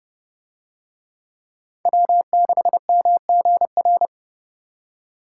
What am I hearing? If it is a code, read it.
Morse code, heard as a keyed tone, about 30 wpm, 700 Hz. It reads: W6MGR